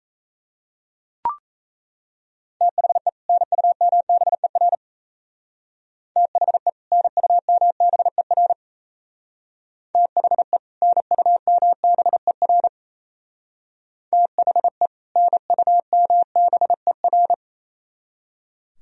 Morse code, heard as a keyed tone, about 34 wpm, 700 Hz. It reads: THE NUMBER THE NUMBER THE NUMBER THE NUMBER
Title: Speed-Racing - Sets of 2 Words 28wpm